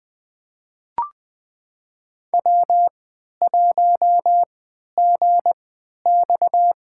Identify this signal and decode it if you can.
Morse code, heard as a keyed tone, about 20 wpm, spaced as for 11 wpm, 700 Hz. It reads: W1GX
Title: USA Call Signs - 4 Characters - 3x Character Spacing - 20wpm